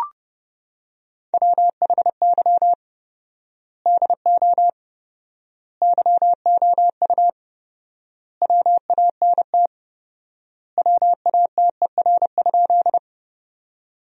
Morse code, heard as a keyed tone, about 30 wpm, 700 Hz. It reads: WHY DO YOU WANT WATER?
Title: 4x Word-Spacing - Sentences from Top 200 Words - 30wpm